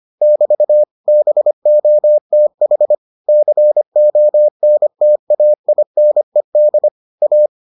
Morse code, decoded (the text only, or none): <BT> BOTH CONTAINED A